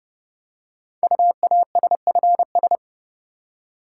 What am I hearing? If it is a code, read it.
Morse code, heard as a keyed tone, about 30 wpm, 700 Hz. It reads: UASFS
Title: Code Groups - 5 Letters - 30wpm